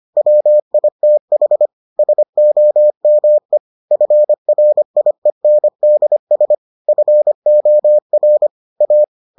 Morse code, heard as a keyed tone, about 25 wpm, 600 Hz. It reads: WITH SOME FRIENDS FOR A